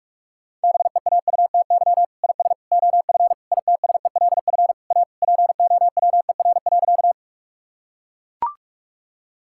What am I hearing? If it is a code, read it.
Morse code, heard as a keyed tone, about 41 wpm, 700 Hz. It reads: BEAUTY IS OF ITSELF A POWER.
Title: Sentences from Top 1000 Words - Review Part 5 - 45wpm